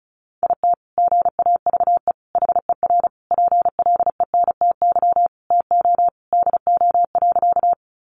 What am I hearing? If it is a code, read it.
Morse code, heard as a keyed tone, about 34 wpm, 700 Hz. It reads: IT GAVE HER PLENTY TO DO.
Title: Sentences from Top 1300 Words 35wpm